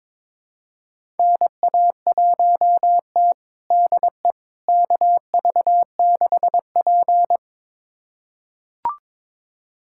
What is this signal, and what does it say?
Morse code, heard as a keyed tone, about 22 wpm, 700 Hz. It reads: NA1T DE KV6P
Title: USA Callsign DE Callsign - 4 Characters - 22wpm